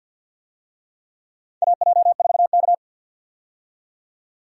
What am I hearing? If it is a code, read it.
Morse code, heard as a keyed tone, about 50 wpm, 700 Hz. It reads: AJ4K
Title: USA Call Signs - 4 Characters - 50wpm